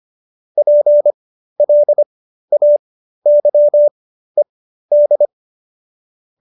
Morse code, heard as a keyed tone, about 25 wpm, spaced as for 13 wpm, 590 Hz. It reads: PLAYED